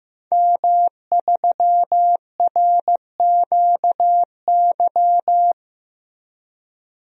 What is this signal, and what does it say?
Morse code, heard as a keyed tone, about 15 wpm, 700 Hz. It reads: M3RQY
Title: DXCC-100 Call Signs - 5 Characters - Repeated 2x - 15wpm